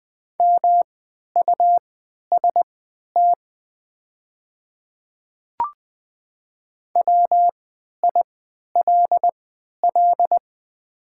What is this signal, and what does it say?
Morse code, heard as a keyed tone, about 20 wpm, spaced as for 11 wpm, 700 Hz. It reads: MUST WILL
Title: 3x Character Spacing - Top 200 Words - 20wpm